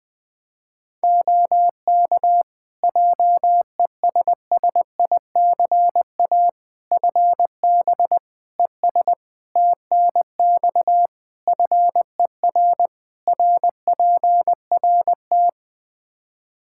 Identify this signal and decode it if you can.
Morse code, heard as a keyed tone, about 21 wpm, 700 Hz. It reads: OK JESSICA FB ES TNX FER RPRT
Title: Practice QSO - Exchange 2 - Info - 20wpm